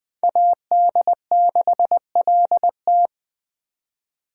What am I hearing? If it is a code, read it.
Morse code, heard as a keyed tone, about 21 wpm, 700 Hz. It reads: AD6LT